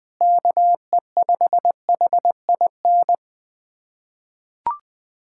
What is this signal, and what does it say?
Morse code, heard as a keyed tone, about 20 wpm, 700 Hz. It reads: KE5HIN
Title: USA Callsign DE Callsign 20wpm